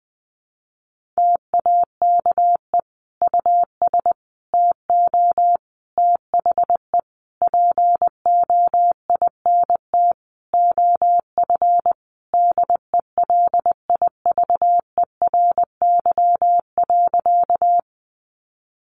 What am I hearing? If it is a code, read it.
Morse code, heard as a keyed tone, about 20 wpm, 700 Hz. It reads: TAKE US TO THE POINT OF DELIVERY.
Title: Sentences from Top 1500 Words 20wpm